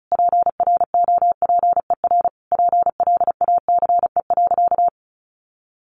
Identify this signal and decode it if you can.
Morse code, heard as a keyed tone, about 34 wpm, 700 Hz. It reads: PROPER PLACE.
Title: Sentences from Top 800 Words 35wpm